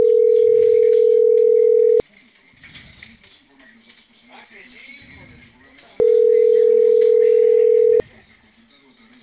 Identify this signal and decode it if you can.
Ringback tone (North America), 440 + 480 Hz, with speech.
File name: CENTRAL CAEA / out agcaea02 - .20430